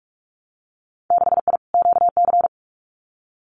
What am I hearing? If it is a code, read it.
Morse code, heard as a keyed tone, about 46 wpm, 700 Hz. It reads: 6I QC
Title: Winter Field Day Exchanges Encore 45wpm